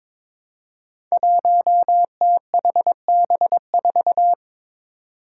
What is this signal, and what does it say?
Morse code, heard as a keyed tone, about 22 wpm, 700 Hz. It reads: 1THB4
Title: Five Challenging Letter-Number Combinations 22wpm